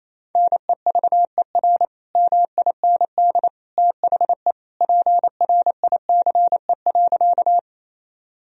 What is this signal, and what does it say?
Morse code, heard as a keyed tone, about 28 wpm, 700 Hz. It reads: NEVER MIND THE PRICE.
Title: Sentences from Top 1000 Words - Review Part 4 - 28wpm